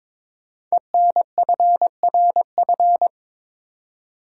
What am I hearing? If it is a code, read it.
Morse code, heard as a keyed tone, about 22 wpm, 700 Hz. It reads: ENFRF
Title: Code Groups - 5 Letters - 22wpm